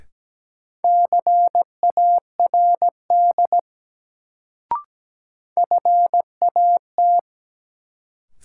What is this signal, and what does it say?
Morse code, heard as a keyed tone, about 17 wpm, 700 Hz. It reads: CARD FAT
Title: Sentences from Top 600 Words - New Word Drill - 17wpm